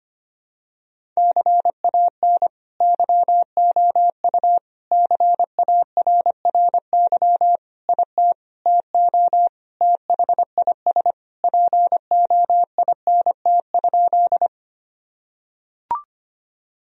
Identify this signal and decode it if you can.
Morse code, heard as a keyed tone, about 25 wpm, 700 Hz. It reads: CAN YOU CARRY IT TO THIS POINT?
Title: Sentences from Top 1000 Words - Review Part 1 - 25wpm